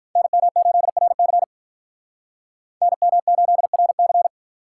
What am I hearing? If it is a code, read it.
Morse code, heard as a keyed tone, about 45 wpm, 700 Hz. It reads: NM8RC NM8RC